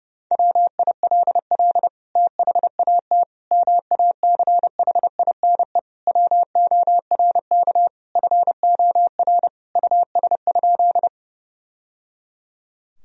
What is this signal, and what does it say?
Morse code, heard as a keyed tone, about 31 wpm, 700 Hz. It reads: WILL THAT MACHINE WORK FOR US?
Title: Sentences from Top 1000 Words - Review Part 2 - 30wpm